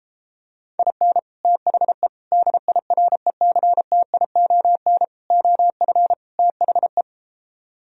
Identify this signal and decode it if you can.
Morse code, heard as a keyed tone, about 33 wpm, 700 Hz. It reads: IN THE DIRECTION OF THE